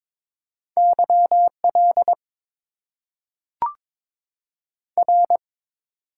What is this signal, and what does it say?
Morse code, heard as a keyed tone, about 22 wpm, 700 Hz. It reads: YL R
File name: Top 90 QSO Elements - 8x Character Spacing - 22wpm